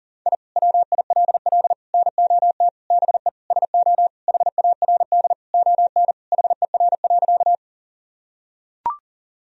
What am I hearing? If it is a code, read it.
Morse code, heard as a keyed tone, about 40 wpm, 700 Hz. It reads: I WILL NOT BE SO HARD ON HER.
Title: Sentences from Top 1000 Words - Review Part 1 - 40wpm